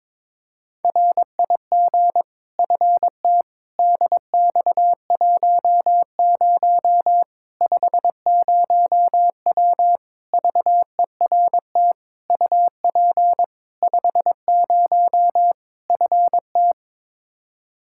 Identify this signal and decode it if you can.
Morse code, heard as a keyed tone, about 22 wpm, 700 Hz. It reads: RIG FT DX10 50W VERT UP 50 FT
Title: Practice QSO - Exchange 2 - Info - 22wpm